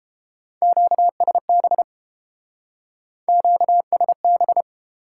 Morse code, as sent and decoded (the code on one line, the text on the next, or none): --.- ... -... / --.- ... -...
QSB QSB